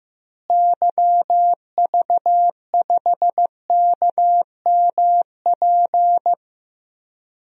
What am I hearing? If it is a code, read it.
Morse code, heard as a keyed tone, about 15 wpm, 700 Hz. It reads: YV5KMP
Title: DXCC-100 Call Signs - 6 Characters - Repeated 2x - 15wpm